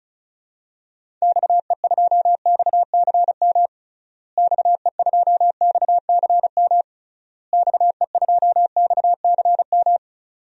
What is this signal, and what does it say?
Morse code, heard as a keyed tone, about 34 wpm, 700 Hz. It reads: XE2XCM XE2XCM XE2XCM
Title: DXCC-100 Call Signs - 6 Characters - Repeated 3x - 35wpm